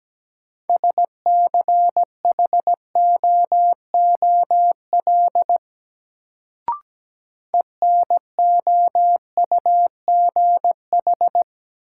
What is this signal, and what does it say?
Morse code, heard as a keyed tone, about 17 wpm, 700 Hz. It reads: SCHOOL ENOUGH
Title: Sets of 2 Words - New Word Drill - 17wpm